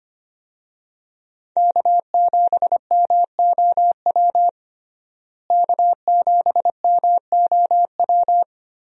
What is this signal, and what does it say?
Morse code, heard as a keyed tone, about 25 wpm, 700 Hz. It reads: K7MOW K7MOW
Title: USA Call Signs - 5 Characters - Repeated 2x - 25wpm